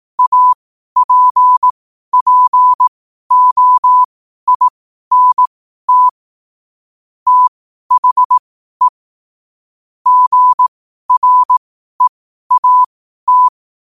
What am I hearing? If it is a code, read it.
Morse code, heard as a keyed tone, about 18 wpm, spaced as for 12 wpm, 1000 Hz. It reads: APPOINT THE GREAT
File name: qotdfiles / 12wpm